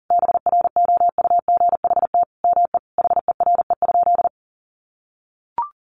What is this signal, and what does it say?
Morse code, heard as a keyed tone, about 40 wpm, 700 Hz. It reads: BROUGHT ME HERE?
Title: Sentences from Top 1200 Words 40wpm